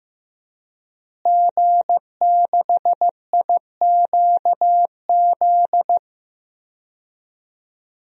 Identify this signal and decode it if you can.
Morse code, heard as a keyed tone, about 16 wpm, 700 Hz. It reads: G6IQZ
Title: DXCC-100 Call Signs - 5 Characters - Repeated 2x - 15wpm